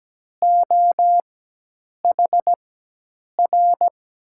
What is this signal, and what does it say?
Morse code, heard as a keyed tone, about 17 wpm, spaced as for 8 wpm, 700 Hz. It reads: OHR
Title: USA Call Signs - 6 Characters - 4x Character Spacing - 17wpm